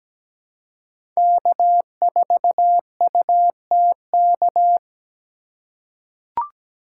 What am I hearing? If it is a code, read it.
Morse code, heard as a keyed tone, about 17 wpm, 700 Hz. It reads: K4UTK
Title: USA Call Signs - 5 Characters - 4x Character Spacing - 17wpm